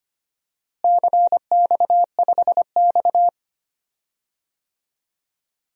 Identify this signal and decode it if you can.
Morse code, heard as a keyed tone, about 25 wpm, 700 Hz. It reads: CX5X
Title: DXCC-100 Call Signs - 4 Characters - Repeated 2x - 25wpm